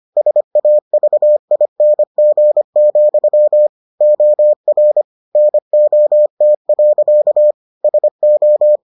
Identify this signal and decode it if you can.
Morse code, heard as a keyed tone, about 25 wpm, 600 Hz. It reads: SAVING, OR NOT. SO